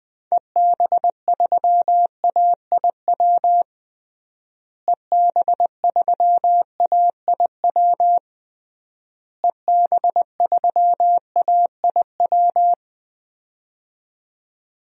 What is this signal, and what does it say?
Morse code, heard as a keyed tone, about 20 wpm, 700 Hz. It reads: EB3AIW EB3AIW EB3AIW
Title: DXCC-100 Call Signs - 6 Characters - Repeated 3x - 20wpm